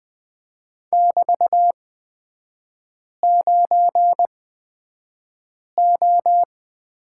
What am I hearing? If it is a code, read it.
Morse code, heard as a keyed tone, about 20 wpm, spaced as for 5 wpm, 700 Hz. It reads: <BT>9O
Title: 224 Lesson 36e - BT - Three Characters 20wpm